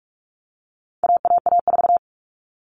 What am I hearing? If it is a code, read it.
Morse code, heard as a keyed tone, about 44 wpm, 700 Hz. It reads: AAA4
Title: All Cut Numbers - 4-Digit 45wpm